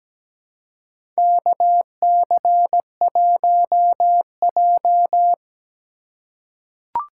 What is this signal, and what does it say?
Morse code, heard as a keyed tone, about 17 wpm, 700 Hz. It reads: KC1J
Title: USA Call Signs - 4 Characters - 4x Character Spacing - 17wpm